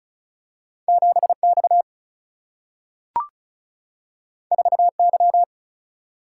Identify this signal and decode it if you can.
Morse code, heard as a keyed tone, about 36 wpm, 700 Hz. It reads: 7X 4Y
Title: Single-Letter Single-Number 35wpm